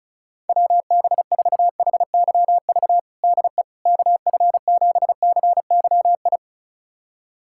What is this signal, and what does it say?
Morse code, heard as a keyed tone, about 34 wpm, 700 Hz. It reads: WB4HYV DE KF7CYI
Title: USA Callsign DE Callsign - 6 Characters - 35wpm